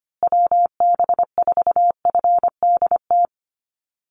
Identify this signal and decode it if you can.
Morse code, heard as a keyed tone, about 25 wpm, 700 Hz. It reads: WB4FDT
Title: Call Signs Repeated 3x - 25wpm